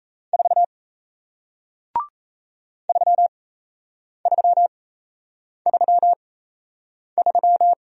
Morse code, heard as a keyed tone, about 38 wpm, spaced as for 8 wpm, 700 Hz. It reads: 4 3333